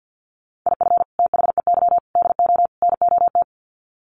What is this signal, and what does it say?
Morse code, heard as a keyed tone, about 50 wpm, 700 Hz. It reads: IF THEY DO NOT
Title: Sets of 4 Words 50wpm